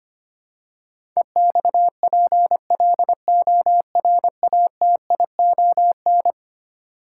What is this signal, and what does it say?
Morse code, heard as a keyed tone, about 25 wpm, 700 Hz. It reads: EXPLORATION
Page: morse-code-ninja-practice-25wpm